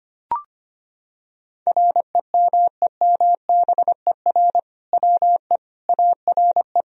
Morse code, heard as a keyed tone, about 25 wpm, 700 Hz. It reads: REMEMBER WE ARE
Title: Sentences from Top 1000 Words - Review Part 3 - 25wpm